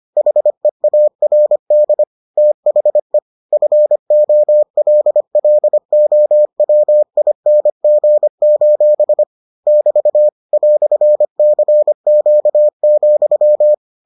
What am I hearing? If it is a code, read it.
Morse code, heard as a keyed tone, about 25 wpm, 600 Hz. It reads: HEARD THE FOLLOWING: <BT> "CQ,